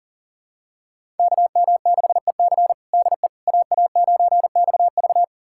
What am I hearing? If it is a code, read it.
Morse code, heard as a keyed tone, about 40 wpm, 700 Hz. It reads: KM6EC DE AA9XV
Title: USA Callsign DE Callsign - 5 Characters - 40wpm